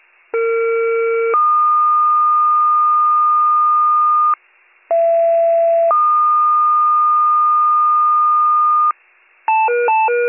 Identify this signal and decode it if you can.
Two-tone page, heard as a one-second tone, then a three-second tone, about 460 then 1220 Hz, about 670 then 1220 Hz. It is heard from 0.3 to 8.9 s.